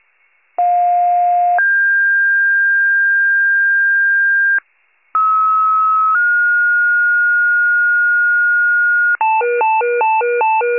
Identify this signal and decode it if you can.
Two-tone page, heard as a one-second tone, then a three-second tone, about 710 then 1600 Hz, about 1290 then 1430 Hz, with noise. It is heard from 0.6 to 9.2 s.
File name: pager / 84 FIRE / wcalert